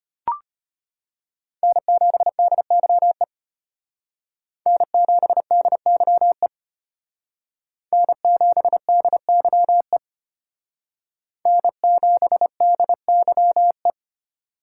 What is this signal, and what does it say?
Morse code, heard as a keyed tone, about 32 wpm, 700 Hz. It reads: N7DYE N7DYE N7DYE N7DYE